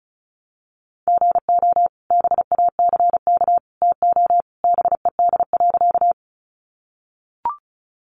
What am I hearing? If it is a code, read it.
Morse code, heard as a keyed tone, about 34 wpm, 700 Hz. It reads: GO BACK TO BED.
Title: Sentences from Top 500 Words 35wpm